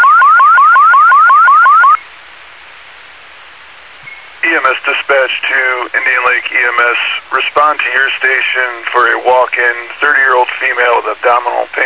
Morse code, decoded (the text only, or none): none